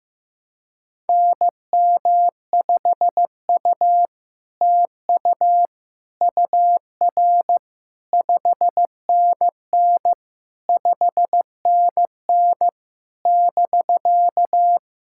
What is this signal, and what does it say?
Morse code, heard as a keyed tone, about 15 wpm, 700 Hz. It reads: NM5U TU UR 5NN 5NN <BK>
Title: SOTA QSO Practice 15wpm